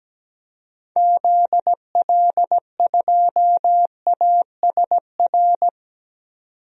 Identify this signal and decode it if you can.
Morse code, heard as a keyed tone, about 17 wpm, 700 Hz. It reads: ZL2ASR
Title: DXCC-100 Call Signs - 6 Characters - 17wpm